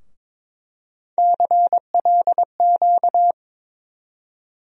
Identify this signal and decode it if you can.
Morse code, heard as a keyed tone, about 22 wpm, 700 Hz. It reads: CLQ
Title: USA Call Sign Postfixes - 3 Letters 22wpm